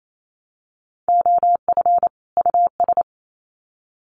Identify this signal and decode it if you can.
Morse code, heard as a keyed tone, about 28 wpm, 700 Hz. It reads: OF US
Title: Sets of 2 Words 28wpm